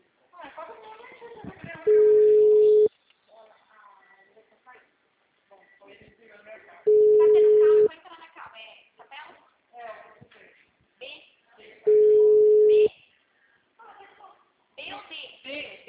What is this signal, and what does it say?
Ringback tone (Europe, 425 Hz), with speech.
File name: out- -2016- - - .6735